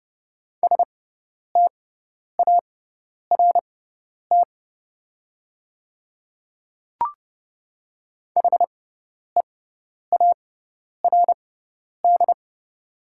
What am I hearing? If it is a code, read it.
Morse code, heard as a keyed tone, about 31 wpm, spaced as for 10 wpm, 700 Hz. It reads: START HEARD
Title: 6x Character Spacing - Top 500 Words - 30wpm